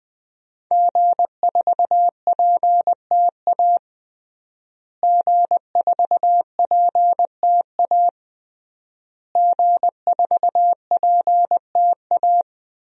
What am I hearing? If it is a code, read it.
Morse code, heard as a keyed tone, about 20 wpm, 700 Hz. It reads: G4PTA G4PTA G4PTA
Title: DXCC-100 Call Signs - 5 Characters - Repeated 3x - 20wpm